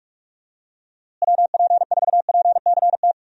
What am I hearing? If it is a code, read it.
Morse code, heard as a keyed tone, about 46 wpm, 700 Hz. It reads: WP4PCT
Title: USA Call Signs - 6 Characters - Repeated 3x - 45wpm